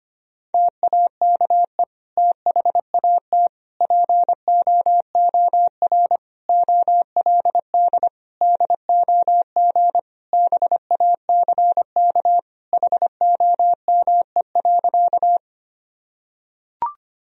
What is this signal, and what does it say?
Morse code, heard as a keyed tone, about 25 wpm, 700 Hz. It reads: TAKE THAT POOR OLD DOG BACK HOME.
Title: Sentences from Top 1000 Words - Review Part 2 - 25wpm